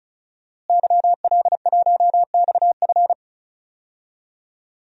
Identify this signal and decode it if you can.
Morse code, heard as a keyed tone, about 34 wpm, 700 Hz. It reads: YL1XF
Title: DXCC-100 Call Signs - 5 Characters - Repeated 3x - 35wpm